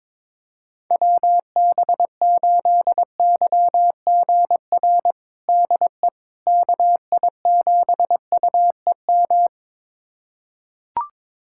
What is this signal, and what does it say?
Morse code, heard as a keyed tone, about 22 wpm, 700 Hz. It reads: WB8YGR DE KI7UEM
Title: USA Callsign DE Callsign - 6 Characters - 22wpm